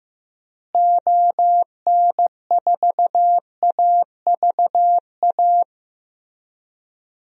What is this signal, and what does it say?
Morse code, heard as a keyed tone, about 15 wpm, 700 Hz. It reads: ON4AVA
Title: DXCC-100 Call Signs - 6 Characters - Repeated 3x - 15wpm